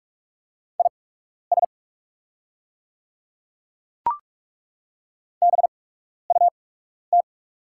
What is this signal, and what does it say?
Morse code, heard as a keyed tone, about 43 wpm, spaced as for 12 wpm, 690 Hz. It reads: IS BUT
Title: 8x Character Spacing - Top 100 Words - 45wpm